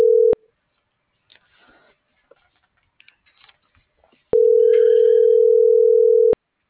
Ringback tone (North America), 440 + 480 Hz.